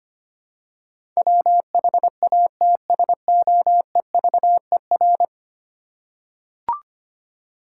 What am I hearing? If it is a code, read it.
Morse code, heard as a keyed tone, about 25 wpm, 700 Hz. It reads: WHATSOEVER
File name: 10 Letter Words - First 3 Letters Given - 25wpm